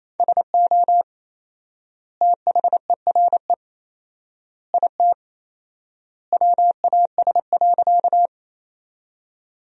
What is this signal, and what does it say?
Morse code, heard as a keyed tone, about 28 wpm, 700 Hz. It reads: SO THERE IT WAS.
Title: 4x Word-Spacing - Sentences from Top 100 Words - 28wpm